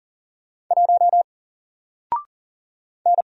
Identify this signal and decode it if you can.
Morse code, heard as a keyed tone, about 40 wpm, 700 Hz. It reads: 1 N
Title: Single Letter-Number 40wpm